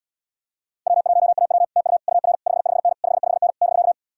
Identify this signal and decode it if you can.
Morse code, heard as a keyed tone, about 77 wpm, 700 Hz. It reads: W1AW TU UR 55N 55N <BK>